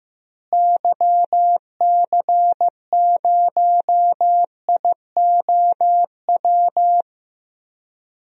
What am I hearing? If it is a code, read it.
Morse code, heard as a keyed tone, about 15 wpm, 700 Hz. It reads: YC0IOW
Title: DXCC-100 Call Signs - 6 Characters - Repeated 3x - 15wpm